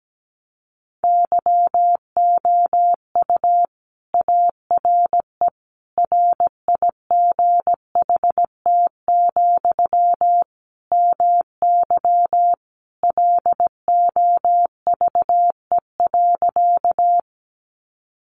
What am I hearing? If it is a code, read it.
Morse code, heard as a keyed tone, about 17 wpm, 700 Hz. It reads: YOU ARE RIGHT, MY LOVE.
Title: Sentences from Top 400 Words 17wpm